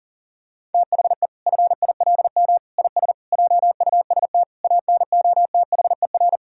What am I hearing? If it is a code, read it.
Morse code, heard as a keyed tone, about 40 wpm, 700 Hz. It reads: THE FILM IS JUST ANOTHER
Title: Sentences from Top 1000 Words - Review Part 3 - 40wpm